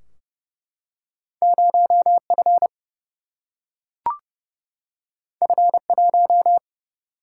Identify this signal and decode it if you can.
Morse code, heard as a keyed tone, about 30 wpm, 700 Hz. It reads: 0F F1
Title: Single-Letter Single-Number 30wpm